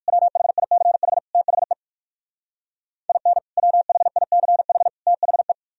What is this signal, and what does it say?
Morse code, heard as a keyed tone, about 58 wpm, 700 Hz. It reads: WHICH THE IN WHICH THE